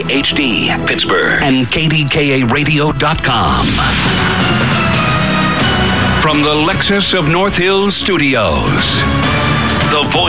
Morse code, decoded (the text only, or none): none